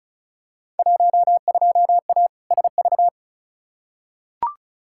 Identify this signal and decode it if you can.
Morse code, heard as a keyed tone, about 36 wpm, 700 Hz. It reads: 12A SV
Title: Field Day Exchanges Encore - 35wpm